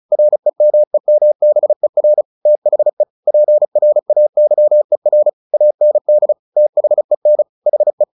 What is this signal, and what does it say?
Morse code, heard as a keyed tone, about 34 wpm, 600 Hz. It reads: REMEMBER THE PRAYER AND THEN HE
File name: the old man and the sea / the old man and the sea 1 35wpm